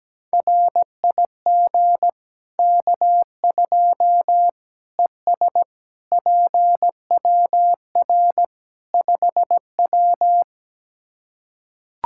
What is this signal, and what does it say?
Morse code, heard as a keyed tone, about 17 wpm, 700 Hz. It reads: RIG K2 ES PWR 5W
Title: Practice QSO - Exchange 2 - Info - 17wpm